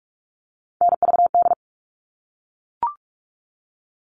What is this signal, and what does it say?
Morse code, heard as a keyed tone, about 44 wpm, 700 Hz. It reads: NVD
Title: All Cut Numbers - 3-Digit 45wpm